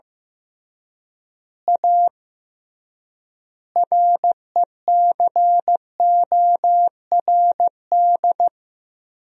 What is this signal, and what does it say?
Morse code, heard as a keyed tone, about 15 wpm, 700 Hz. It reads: A RECORD
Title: 3x Word-Spacing - Sentences from Top 500 Words - 15wpm